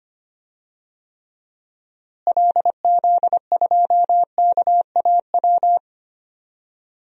Morse code, heard as a keyed tone, about 25 wpm, 700 Hz. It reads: LZ2KAW
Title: DXCC-100 Call Signs - 6 Characters - Repeated 2x - 25wpm